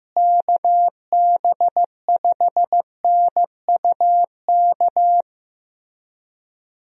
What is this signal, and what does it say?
Morse code, heard as a keyed tone, about 16 wpm, 700 Hz. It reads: KB5NUK
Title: USA Call Signs - 6 Characters - Repeated 2x - 15wpm